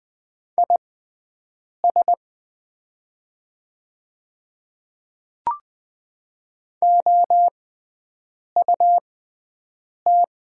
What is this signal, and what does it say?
Morse code, heard as a keyed tone, about 20 wpm, spaced as for 7 wpm, 700 Hz. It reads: IS OUT
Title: 6x Character Spacing - Top 100 Words - 20wpm